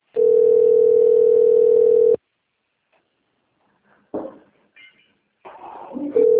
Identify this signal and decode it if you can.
Ringback tone (North America), 440 + 480 Hz.